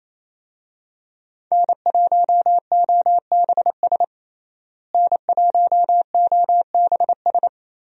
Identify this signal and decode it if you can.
Morse code, heard as a keyed tone, about 28 wpm, 700 Hz. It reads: N1OBS N1OBS
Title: USA Call Signs - 5 Characters - Repeated 3x - 28wpm